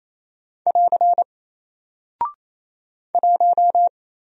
Morse code, heard as a keyed tone, about 27 wpm, 700 Hz. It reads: <AR> 1
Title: Single Character (Letter:Number:Punctuation:Prosign) Alt 28wpm